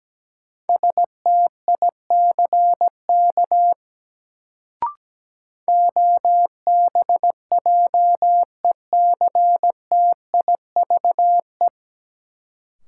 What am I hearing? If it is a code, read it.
Morse code, heard as a keyed tone, about 17 wpm, 700 Hz. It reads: STICK OBJECTIVE
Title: Sentences from Top 1200 Words - New Word Drill - 17wpm